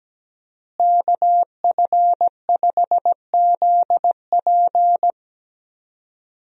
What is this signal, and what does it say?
Morse code, heard as a keyed tone, about 17 wpm, 700 Hz. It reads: KF5ZP
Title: USA Call Signs - 5 Characters - Repeated 3x - 17wpm